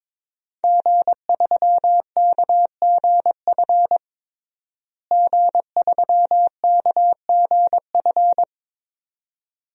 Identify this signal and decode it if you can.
Morse code, heard as a keyed tone, about 22 wpm, 700 Hz. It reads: G3KGF G3KGF
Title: DXCC-100 Call Signs - 5 Characters - Repeated 2x - 22wpm